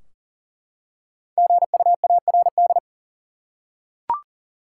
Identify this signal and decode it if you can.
Morse code, heard as a keyed tone, about 37 wpm, 700 Hz. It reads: GUARD
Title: Five Letter Words - 40wpm